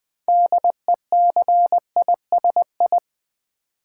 Morse code, heard as a keyed tone, about 20 wpm, 700 Hz. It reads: DECISI
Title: Build A Word - 3 Letters At A Time - 8-Letter Words - 20wpm